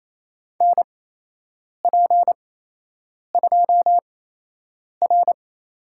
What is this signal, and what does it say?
Morse code, heard as a keyed tone, about 28 wpm, spaced as for 8 wpm, 700 Hz. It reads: NP2R
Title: USA Call Signs - 5 Characters - 8x Character Spacing - 28wpm